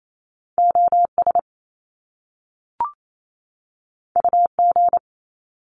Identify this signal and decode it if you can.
Morse code, heard as a keyed tone, about 28 wpm, 700 Hz. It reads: OS UG
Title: Top Two Letter Combinations in Words 28wpm